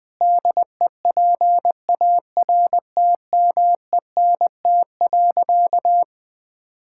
Morse code, heard as a keyed tone, about 20 wpm, 700 Hz. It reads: DEPARTMENT.
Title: Sentences with 4-Word Phrases - 20wpm